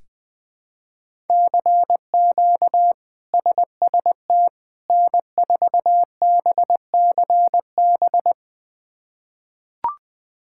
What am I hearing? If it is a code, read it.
Morse code, heard as a keyed tone, about 21 wpm, 700 Hz. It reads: CQ SST N4BCB
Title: K1USN SST QSO Practice - 20wpm